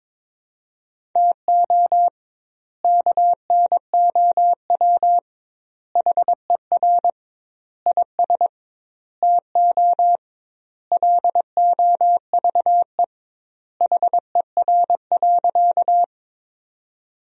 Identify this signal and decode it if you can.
Morse code, heard as a keyed tone, about 21 wpm, 700 Hz. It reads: TO KNOW HER IS TO LOVE HER.
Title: 2x Word-Spacing - Sentences from Top 400 Words - 22wpm